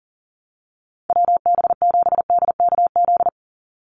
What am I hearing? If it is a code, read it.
Morse code, heard as a keyed tone, about 40 wpm, 700 Hz. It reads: WB7DKZ
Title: Call Signs - 40wpm